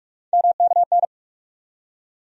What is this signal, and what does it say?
Morse code, heard as a keyed tone, about 46 wpm, 700 Hz. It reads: MKN
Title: USA Call Sign Postfixes - 3 Letters 45wpm